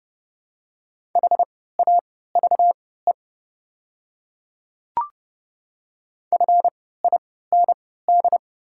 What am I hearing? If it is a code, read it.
Morse code, heard as a keyed tone, about 32 wpm, spaced as for 17 wpm, 700 Hz. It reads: HAVE FIND
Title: 3x Character Spacing - Top 100 Words - 30wpm